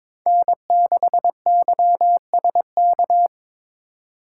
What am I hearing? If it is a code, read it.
Morse code, heard as a keyed tone, about 26 wpm, 700 Hz. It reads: N6YSK
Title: Speed Racing - USA 1x3 Call Signs 22wpm